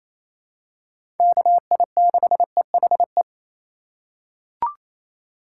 Morse code, heard as a keyed tone, about 28 wpm, 700 Hz. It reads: KI6EHE
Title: USA Call Signs - 6 Characters - 4x Character Spacing - 28wpm